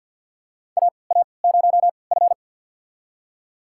Morse code, heard as a keyed tone, about 52 wpm, spaced as for 29 wpm, 700 Hz. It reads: AA0F
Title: USA Call Signs - 3x Character Spacing - 50wpm